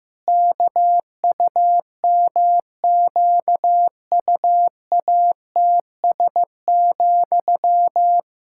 Morse code, heard as a keyed tone, about 15 wpm, 700 Hz. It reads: KUMQUATS,
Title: Pangram Challenge v2 - 15wpm